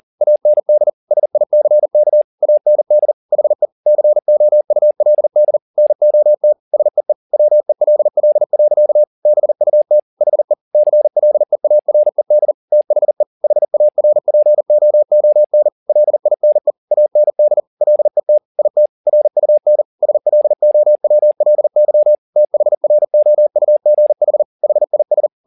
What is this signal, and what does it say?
Morse code, heard as a keyed tone, about 40 wpm, 600 Hz. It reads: AND SICK AND HE COULD NOT SEE WELL. BUT HE CLEARED THE HARPOON LINE AND LET IT RUN SLOWLY THROUGH HIS
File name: the old man and the sea 2 40wpm